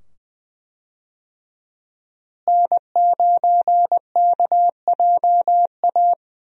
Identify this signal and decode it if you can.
Morse code, heard as a keyed tone, about 20 wpm, 700 Hz. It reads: N9KJA
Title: USA Call Signs - 5 Characters - Repeated 2x - 20wpm